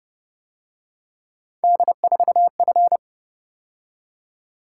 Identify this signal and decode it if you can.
Morse code, heard as a keyed tone, about 30 wpm, 700 Hz. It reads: D4F
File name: Speed-Racing - Three Letter-Number Combinations - 25wpm